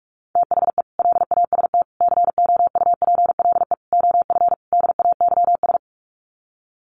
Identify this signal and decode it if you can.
Morse code, heard as a keyed tone, about 44 wpm, 700 Hz. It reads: THE LAST COUPLE OF DAYS
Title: Sets of 5 Words Encore 45wpm